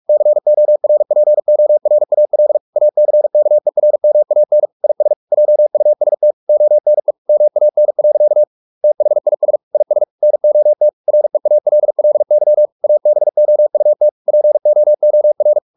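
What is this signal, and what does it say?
Morse code, heard as a keyed tone, about 44 wpm, 600 Hz. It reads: CORPORAL ACKERMAN IS JUST ONE MAN. THIS IS NOT REALLY ABOUT POOR
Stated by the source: Morse practice at 45 wpm, the novel Never by Ken Follett in English, from CW med Gnister.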